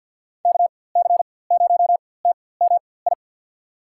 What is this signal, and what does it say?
Morse code, heard as a keyed tone, about 52 wpm, spaced as for 23 wpm, 700 Hz. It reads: KC0TMI
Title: USA Call Signs - 6 Characters - 4x Character Spacing - 50wpm